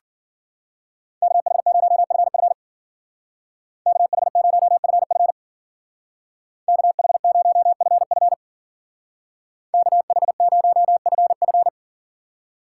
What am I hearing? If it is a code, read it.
Morse code, heard as a keyed tone, about 48 wpm, 700 Hz. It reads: KH0FF KH0FF KH0FF KH0FF
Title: Speed Racing - USA 2x2 Call Signs 40wpm